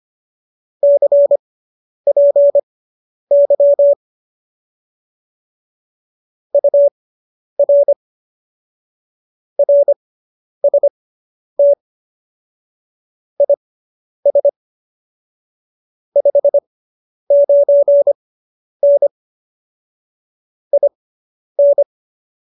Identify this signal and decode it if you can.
Morse code, heard as a keyed tone, about 25 wpm, spaced as for 10 wpm, 570 Hz. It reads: CPY UR RST IS 59N IN